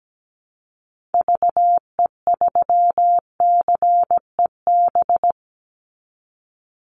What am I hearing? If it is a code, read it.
Morse code, heard as a keyed tone, about 17 wpm, 700 Hz. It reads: VE3CEB
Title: Call Signs Repeated 2x - 17wpm